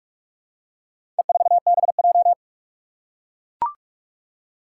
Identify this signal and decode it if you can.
Morse code, heard as a keyed tone, about 44 wpm, 700 Hz. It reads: E4BJ
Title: Four Challenging Letter-Number Combinations 45wpm